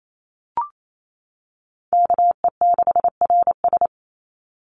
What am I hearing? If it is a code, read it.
Morse code, heard as a keyed tone, about 28 wpm, 700 Hz. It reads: KE6RS
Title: NA SOTA Chaser Callsigns 28wpm